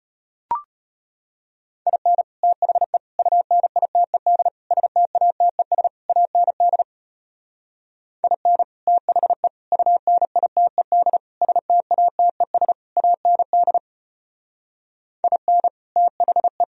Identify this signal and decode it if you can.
Morse code, heard as a keyed tone, about 33 wpm, 700 Hz. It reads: IN THE UNITED STATES AND IN THE UNITED STATES AND IN THE